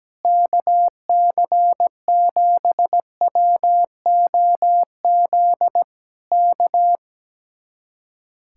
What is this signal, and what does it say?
Morse code, heard as a keyed tone, about 17 wpm, 700 Hz. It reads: KC7WOZ K
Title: Speed-Racing Common QSO Phrases Encore - 17wpm